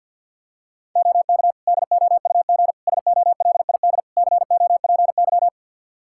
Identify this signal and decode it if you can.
Morse code, heard as a keyed tone, about 52 wpm, 700 Hz. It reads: OK DOUG SOLID COPY
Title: Practice QSO - Exchange 3 - More Info - 50wpm